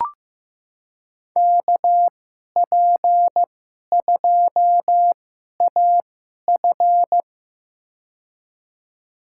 Morse code, heard as a keyed tone, about 15 wpm, spaced as for 11 wpm, 700 Hz. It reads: KP2AF
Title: USA Call Signs - 5 Characters - 2x Character Spacing - 15wpm